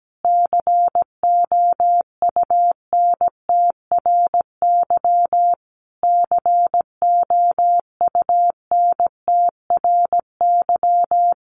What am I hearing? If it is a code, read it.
Morse code, heard as a keyed tone, about 17 wpm, 700 Hz. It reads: COUNTRY COUNTRY
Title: Top 500 Words Repeated 17wpm